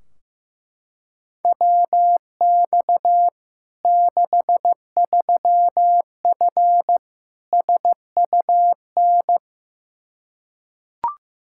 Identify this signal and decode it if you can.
Morse code, heard as a keyed tone, about 16 wpm, 700 Hz. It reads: WX 63F SUN
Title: Practice QSO - Exchange 2 - Info - 15wpm